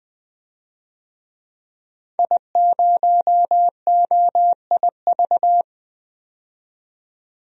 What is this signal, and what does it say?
Morse code, heard as a keyed tone, about 20 wpm, 700 Hz. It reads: I0OIV
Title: DXCC-100 Call Signs - 5 Characters - 20wpm